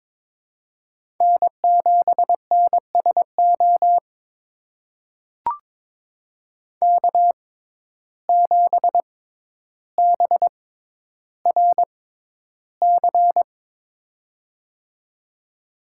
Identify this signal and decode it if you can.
Morse code, heard as a keyed tone, about 23 wpm, spaced as for 11 wpm, 700 Hz. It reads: N7NSO K7BRC
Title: USA Call Signs - 5 Characters - 6x Character Spacing - 22wpm